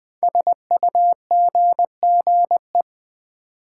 Morse code, heard as a keyed tone, about 20 wpm, 700 Hz. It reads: SUGGE
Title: Build A Word - 5 Letters At A Time - 10-Letter Words - 20wpm